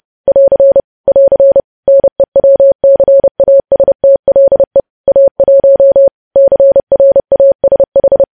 Morse code, heard as a keyed tone, about 30 wpm, 550 Hz. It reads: <AR> <AR> NEWCASTLE A1 CRASH
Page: news headlines in morse code at 30 wpm